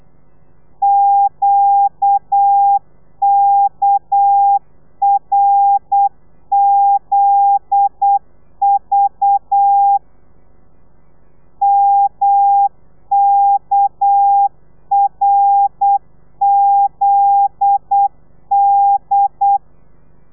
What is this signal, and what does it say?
Morse code, heard as a keyed tone, about 8 wpm, 800 Hz. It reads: QKRZV MKRZD
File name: telegrafia / i2tfj / Lez9 35